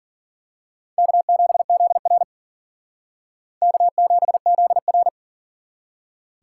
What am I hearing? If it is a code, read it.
Morse code, heard as a keyed tone, about 42 wpm, 700 Hz. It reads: K7ZR K7ZR
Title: Speed Racing - USA 1x2 Call Signs 40wpm